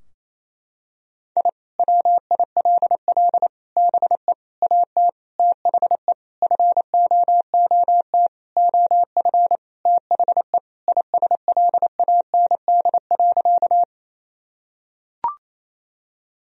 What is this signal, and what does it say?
Morse code, heard as a keyed tone, about 28 wpm, 700 Hz. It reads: I WILL BE AT THE FOOT OF THE ISLAND.
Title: Sentences from Top 1000 Words - Review Part 3 - 28wpm